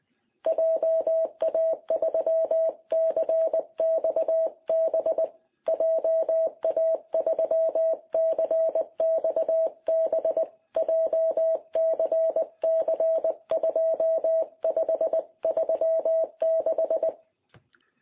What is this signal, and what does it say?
Morse code, heard as a keyed tone, about 19 wpm, 660 Hz. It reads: JA3CXB JA3CXB JCC2536
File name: jh8cbh / 2021cw